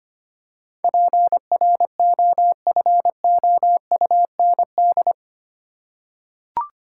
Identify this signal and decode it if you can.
Morse code, heard as a keyed tone, about 25 wpm, 700 Hz. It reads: PROFOUND
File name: Eight Letter Words - 25wpm